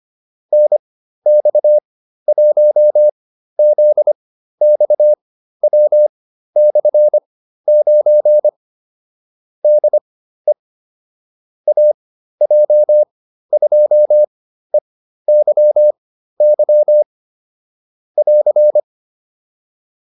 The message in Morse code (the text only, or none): NX1ZXW/9 DE AJ2EYY <AR>